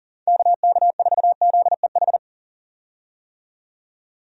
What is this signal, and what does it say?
Morse code, heard as a keyed tone, about 40 wpm, 700 Hz. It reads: KK4ZEH
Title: USA Call Signs - 6 Characters - 40wpm